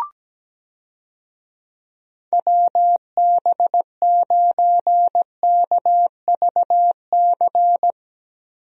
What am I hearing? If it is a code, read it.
Morse code, heard as a keyed tone, about 17 wpm, 700 Hz. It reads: WB9KVC
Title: USA Call Signs - 6 Characters - Repeated 3x - 17wpm